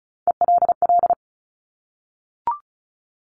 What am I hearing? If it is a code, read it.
Morse code, heard as a keyed tone, about 36 wpm, 700 Hz. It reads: ELL
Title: Top Three Letter Combinations in Words 35wpm